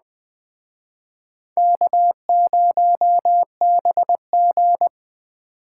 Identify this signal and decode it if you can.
Morse code, heard as a keyed tone, about 20 wpm, 700 Hz. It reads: K0BG
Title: Speed Racing - USA Call Signs 17wpm